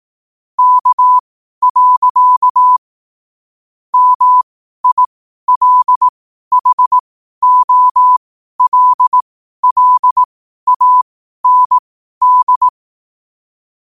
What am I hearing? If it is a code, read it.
Morse code, heard as a keyed tone, about 18 wpm, spaced as for 12 wpm, 1000 Hz. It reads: K. MILHOLLAND
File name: 12wpm